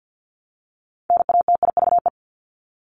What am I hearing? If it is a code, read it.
Morse code, heard as a keyed tone, about 50 wpm, 700 Hz. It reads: NATIVE
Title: Top 1751-2000 Words 50wpm